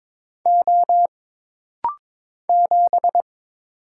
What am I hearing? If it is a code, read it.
Morse code, heard as a keyed tone, about 23 wpm, 700 Hz. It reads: O 7
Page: This rapid-fire timing is geared to push your ICR (Instant Character Recognition) skills.